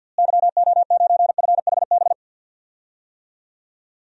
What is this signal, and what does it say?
Morse code, heard as a keyed tone, about 53 wpm, 700 Hz. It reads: YO9FHB
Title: DXCC-100 Call Signs - 6 Characters - Repeated 2x - 50wpm